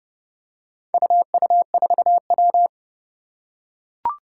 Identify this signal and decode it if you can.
Morse code, heard as a keyed tone, about 30 wpm, 700 Hz. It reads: UU4W
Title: Four Challenging Letter-Number Combinations 30wpm